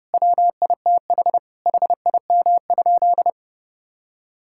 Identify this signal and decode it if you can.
Morse code, heard as a keyed tone, about 30 wpm, 700 Hz. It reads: WITH HIM?
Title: Sentences from Top 1000 Words - Review Part 6 - 30wpm